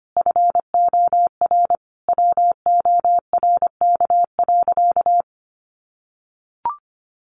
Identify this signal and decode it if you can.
Morse code, heard as a keyed tone, about 25 wpm, 700 Hz. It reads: FOR WORK.
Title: Sentences from Top 700 Words 25wpm